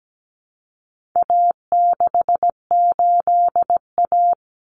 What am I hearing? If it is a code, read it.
Morse code, heard as a keyed tone, about 17 wpm, 700 Hz. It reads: A68A
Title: Common Cut Numbers - 4-Digit 17wpm